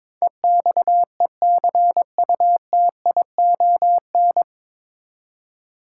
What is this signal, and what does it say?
Morse code, heard as a keyed tone, about 22 wpm, 700 Hz. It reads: EXECUTION
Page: Morse Code Ninja practice 22wpm